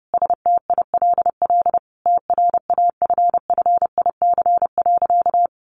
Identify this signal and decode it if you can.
Morse code, heard as a keyed tone, about 30 wpm, 700 Hz. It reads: STILL TRAFFIC.
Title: Sentences from Top 1300 Words 30wpm